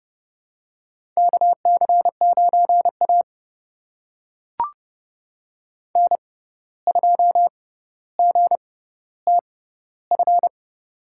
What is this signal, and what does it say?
Morse code, heard as a keyed tone, about 30 wpm, spaced as for 14 wpm, 700 Hz. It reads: KC9A N2GTF
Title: USA Call Signs - 6x Character Spacing - 30wpm